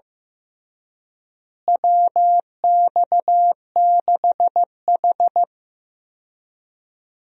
Morse code, heard as a keyed tone, about 15 wpm, 700 Hz. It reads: WX6H
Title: USA Call Signs - 4 Characters - Repeated 2x - 15wpm